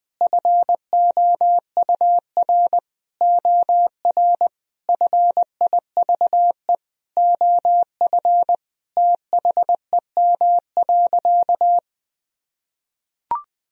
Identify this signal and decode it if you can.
Morse code, heard as a keyed tone, about 20 wpm, 700 Hz. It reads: FOUR OR FIVE OF THEM.
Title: Sentences from Top 1000 Words - Review Part 2 - 20wpm